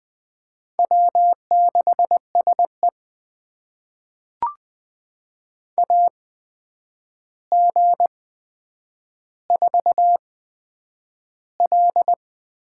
Morse code, heard as a keyed tone, about 20 wpm, spaced as for 9 wpm, 700 Hz. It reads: W6SE AG4L
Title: USA Call Signs - 4 Characters - 8x Character Spacing - 20wpm